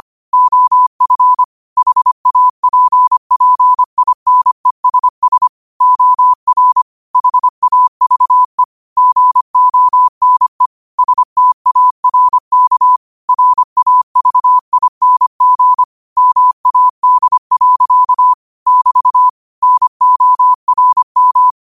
Morse code, heard as a keyed tone, about 25 wpm, 1000 Hz. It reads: OF HAPPINESS OR HAVE GONE STARK RAVING MAD. <BT> NORM
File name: qotdfiles / 25wpm